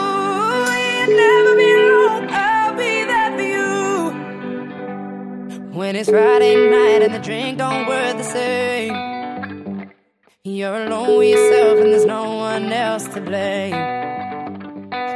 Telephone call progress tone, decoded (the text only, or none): ringback tone